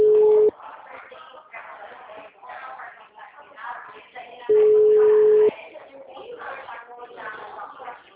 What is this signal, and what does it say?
Ringback tone (Europe, 425 Hz), with speech.
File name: out- -2003- - - .70068